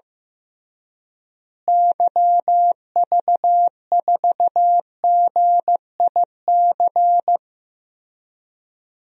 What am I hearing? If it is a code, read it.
Morse code, heard as a keyed tone, about 15 wpm, 700 Hz. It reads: YV4GIC
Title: DXCC-100 Call Signs - 6 Characters - Repeated 3x - 15wpm